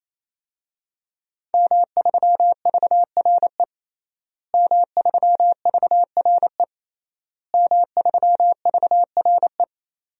Morse code, heard as a keyed tone, about 28 wpm, 700 Hz. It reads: M3VRE M3VRE M3VRE
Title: DXCC-100 Call Signs - 5 Characters - Repeated 3x - 28wpm